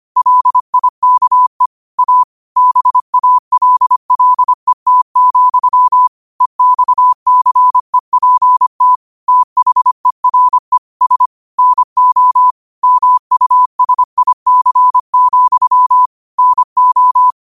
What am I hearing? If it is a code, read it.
Morse code, heard as a keyed tone, about 25 wpm, 1000 Hz. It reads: LIKE A BALLET, EXCEPT THERE S NO MUSIC, NO
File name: qotdfiles / 25wpm